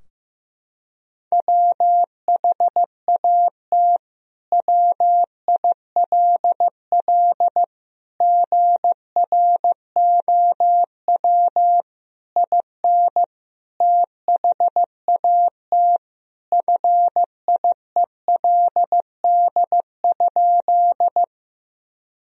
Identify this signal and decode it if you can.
Morse code, heard as a keyed tone, about 16 wpm, 700 Hz. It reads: WHAT WILL GROW IN THAT FIELD?
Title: Sentences from Top 1000 Words - Review Part 2 - 15wpm